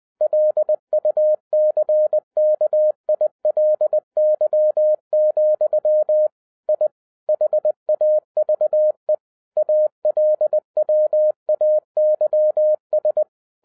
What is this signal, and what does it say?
Morse code, heard as a keyed tone, about 20 wpm, 610 Hz. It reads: LUCKILY, I HAVE ALWAYS
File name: 2012-01-mp3 / n16-THROUGH EIGHT MONTHS-20wpm